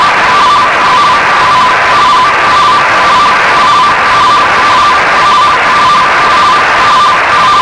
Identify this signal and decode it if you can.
Two-tone teleprinter signal, about 170 Hz shift.
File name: Oscar 10 RTTY 17 Feb 1985 09:15